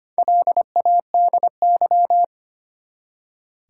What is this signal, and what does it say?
Morse code, heard as a keyed tone, about 25 wpm, 700 Hz. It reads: LADY
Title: Four Letter Words - 25wpm